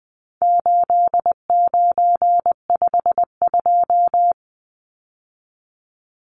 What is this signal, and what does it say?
Morse code, heard as a keyed tone, about 20 wpm, 700 Hz. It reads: 8952
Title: Numbers - 4-Digit 20wpm